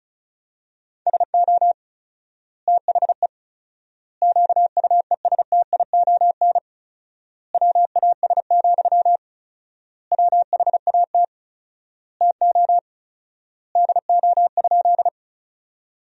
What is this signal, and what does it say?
Morse code, heard as a keyed tone, about 34 wpm, 700 Hz. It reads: SO THE QUESTION WAS, WHAT TO DO?
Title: 4x Word-Spacing - Sentences from Top 400 Words - 35wpm